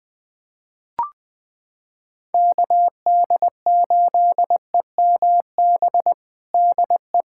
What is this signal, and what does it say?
Morse code, heard as a keyed tone, about 20 wpm, 700 Hz. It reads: KD8EMB DE
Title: USA Callsign DE Callsign 20wpm